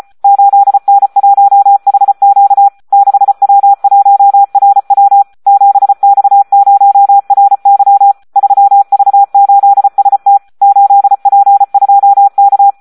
Morse code, heard as a keyed tone, about 34 wpm, 800 Hz. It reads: 8N1HQ 6W1RW 7X0RY 3V8ST 8P2K
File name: Audio practica CW / SD 1 34